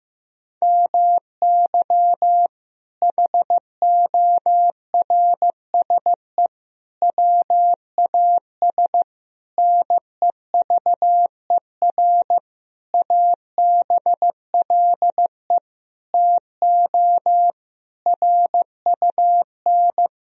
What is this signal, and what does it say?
Morse code, heard as a keyed tone, about 15 wpm, 700 Hz. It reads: MY HORSE WAS NEVER ABLE TO RUN
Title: Sentences from Top 1000 Words - Review Part 2 - 15wpm